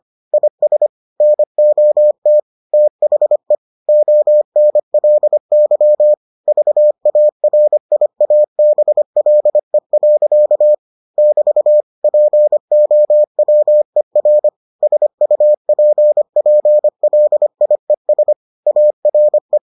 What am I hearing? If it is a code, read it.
Morse code, heard as a keyed tone, about 25 wpm, 600 Hz. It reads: IS NOT THE ONLY VARIABLE. <BT> POWER SUPPLIES ARE